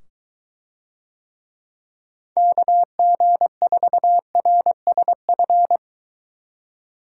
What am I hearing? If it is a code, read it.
Morse code, heard as a keyed tone, about 23 wpm, 700 Hz. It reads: KG4RSF